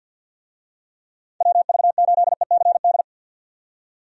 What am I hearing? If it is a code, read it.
Morse code, heard as a keyed tone, about 48 wpm, 700 Hz. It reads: WV8ECD
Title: USA Call Signs - 6 Characters - 50wpm